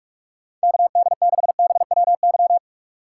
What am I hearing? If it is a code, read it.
Morse code, heard as a keyed tone, about 46 wpm, 700 Hz. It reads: KD6BWY
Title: USA Call Signs - 6 Characters - 4x Character Spacing - 45wpm